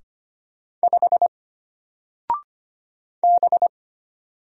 Morse code, heard as a keyed tone, about 25 wpm, 700 Hz. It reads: 5 B
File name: Single-Letter-Number-25wpm